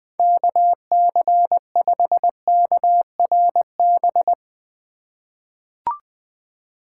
Morse code, heard as a keyed tone, about 20 wpm, 700 Hz. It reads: KC5KRB
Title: USA Call Signs - 6 Characters - 3x Character Spacing - 20wpm